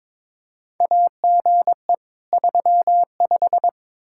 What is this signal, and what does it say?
Morse code, heard as a keyed tone, about 21 wpm, 700 Hz. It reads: AGE 35
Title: Practice QSO - Exchange 3 - More Info - 22wpm